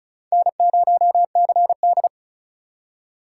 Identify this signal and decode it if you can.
Morse code, heard as a keyed tone, about 34 wpm, 690 Hz. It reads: N0CD
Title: USA Call Signs - 4 Characters - 2x Character Spacing - 35wpm